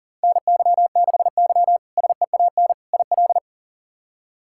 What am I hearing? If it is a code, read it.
Morse code, heard as a keyed tone, about 40 wpm, 700 Hz. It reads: NY6Y SEAN IL
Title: K1USN SST QSO Practice - 40wpm